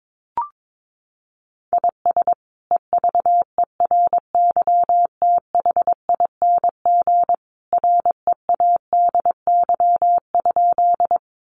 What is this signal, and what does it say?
Morse code, heard as a keyed tone, about 22 wpm, 700 Hz. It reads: IS EVERYTHING READY?
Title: Sentences from Top 700 Words 22wpm